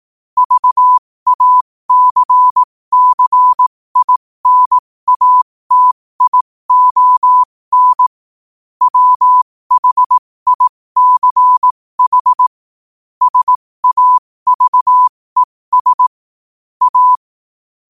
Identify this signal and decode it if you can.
Morse code, heard as a keyed tone, about 18 wpm, spaced as for 15 wpm, 1000 Hz. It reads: VACCINATION WHICH SAVES A
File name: qotdfiles / 15wpm